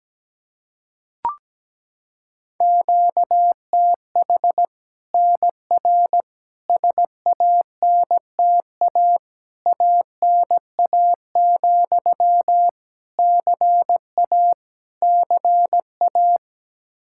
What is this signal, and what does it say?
Morse code, heard as a keyed tone, about 17 wpm, 700 Hz. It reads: QTH NR SANTA ANA, CA CA
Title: Practice QSO - Exchange 1 - Beginning - 17wpm